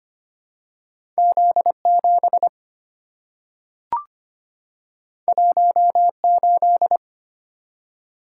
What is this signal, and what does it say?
Morse code, heard as a keyed tone, about 25 wpm, 700 Hz. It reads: Z7 18
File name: Two Letter-Number Combinations - 25wpm